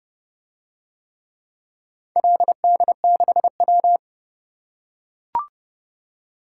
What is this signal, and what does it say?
Morse code, heard as a keyed tone, about 31 wpm, 700 Hz. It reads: LD6W